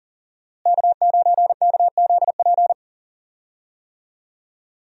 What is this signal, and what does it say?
Morse code, heard as a keyed tone, about 40 wpm, 700 Hz. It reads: K9KZP
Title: USA Call Signs - 5 Characters - Repeated 3x - 40wpm